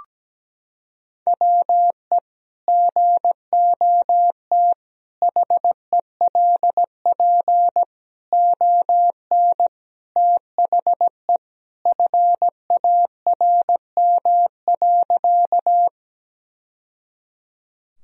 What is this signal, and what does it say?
Morse code, heard as a keyed tone, about 17 wpm, 700 Hz. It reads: WE GOT HELP ON THE FARM.
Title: Sentences from Top 1000 Words - Review Part 2 - 17wpm